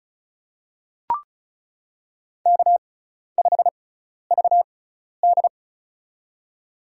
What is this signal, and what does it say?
Morse code, heard as a keyed tone, about 34 wpm, spaced as for 12 wpm, 700 Hz. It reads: K5VD
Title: USA Call Signs - 4 Characters - 6x Character Spacing - 35wpm